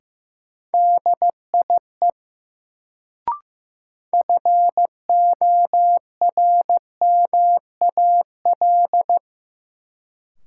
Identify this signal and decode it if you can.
Morse code, heard as a keyed tone, about 15 wpm, 700 Hz. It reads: DIE FORMAL
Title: Sentences from Top 1300 Words - New Word Drill - 15wpm